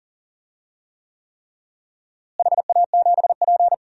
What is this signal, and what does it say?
Morse code, heard as a keyed tone, about 40 wpm, 700 Hz. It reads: HA7P
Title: DXCC-100 Call Signs - 4 Characters - Repeated 3x - 40wpm